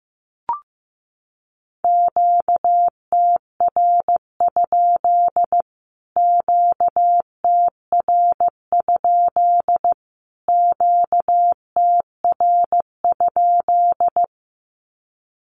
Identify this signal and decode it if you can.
Morse code, heard as a keyed tone, about 15 wpm, 700 Hz. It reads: QTR? QTR? QTR?
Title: Q-Codes Repeated 3x 15wpm